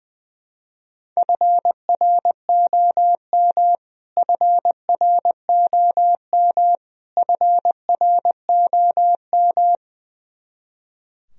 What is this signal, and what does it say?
Morse code, heard as a keyed tone, about 21 wpm, 700 Hz. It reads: FROM FROM FROM
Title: Top 100 Words Repeated 20wpm